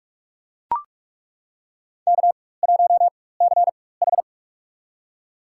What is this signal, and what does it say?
Morse code, heard as a keyed tone, about 42 wpm, spaced as for 20 wpm, 700 Hz. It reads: K1CH
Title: USA Call Signs - 4 Characters - 4x Character Spacing - 45wpm